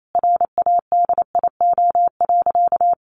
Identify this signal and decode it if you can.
Morse code, heard as a keyed tone, about 28 wpm, 700 Hz. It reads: RADIO.
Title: A Sound Of Thunder 28wpm